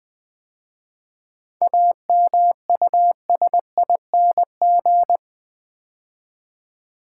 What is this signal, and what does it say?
Morse code, heard as a keyed tone, about 20 wpm, 700 Hz. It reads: AMUSING
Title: 7 Letter Words - First 3 Letters Given - 20wpm